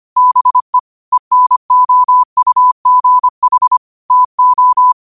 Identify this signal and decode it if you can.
Morse code, heard as a keyed tone, about 25 wpm, 1000 Hz. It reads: BE ENOUGH TO